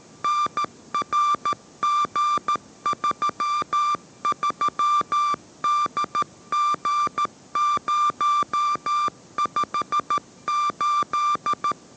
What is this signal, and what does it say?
Morse code, heard as a keyed tone, about 13 wpm, 1180 Hz. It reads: NRG33DG058